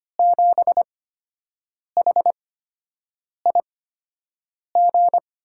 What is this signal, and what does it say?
Morse code, heard as a keyed tone, about 25 wpm, spaced as for 7 wpm, 700 Hz. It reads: 7HIG
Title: USA Call Signs - 6 Characters - 8x Character Spacing - 25wpm